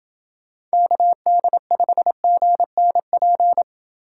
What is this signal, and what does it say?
Morse code, heard as a keyed tone, about 27 wpm, 700 Hz. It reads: KD5GNP